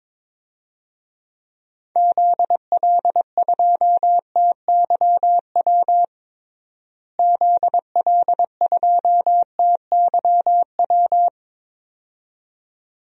Morse code, heard as a keyed tone, about 22 wpm, 700 Hz. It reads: ZL2TYW ZL2TYW
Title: DXCC-100 Call Signs - 6 Characters - Repeated 2x - 22wpm